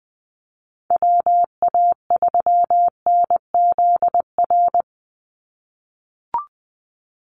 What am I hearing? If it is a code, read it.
Morse code, heard as a keyed tone, about 20 wpm, 700 Hz. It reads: WA3NZR
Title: Call Signs - 20wpm